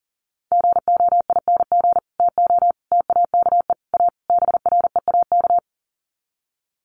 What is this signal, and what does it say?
Morse code, heard as a keyed tone, about 40 wpm, 700 Hz. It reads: GOING TO TAKE A BREAK
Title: Sets of 5 Words 40wpm